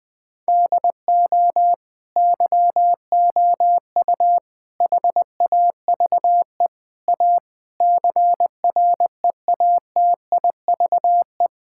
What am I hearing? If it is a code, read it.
Morse code, heard as a keyed tone, about 20 wpm, 700 Hz. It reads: DO YOU HAVE A CREATIVE
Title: Sentences from Top 1000 Words - Review Part 5 - 20wpm